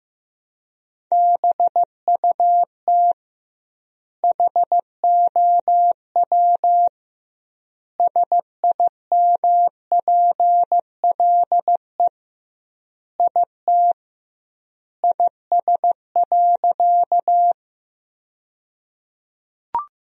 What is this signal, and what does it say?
Morse code, heard as a keyed tone, about 15 wpm, 700 Hz. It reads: BUT HOW SIMPLE IT IS.
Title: 2x Word-Spacing - Sentences from Top 400 Words - 15wpm